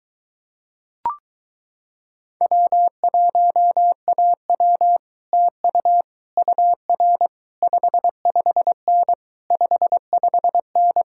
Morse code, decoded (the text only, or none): W1AW TU UR 55N 55N